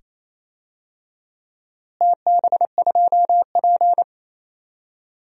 Morse code, heard as a keyed tone, about 28 wpm, 700 Hz. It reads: TB2P
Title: DXCC-100 Call Signs - 4 Characters - Repeated 2x - 28wpm